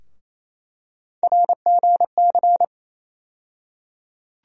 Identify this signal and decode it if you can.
Morse code, heard as a keyed tone, about 28 wpm, 700 Hz. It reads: RGC
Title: USA Call Sign Postfixes - 3 Letters 28wpm